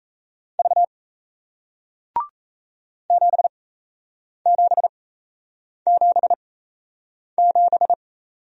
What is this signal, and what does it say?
Morse code, heard as a keyed tone, about 36 wpm, spaced as for 8 wpm, 700 Hz. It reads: V 7777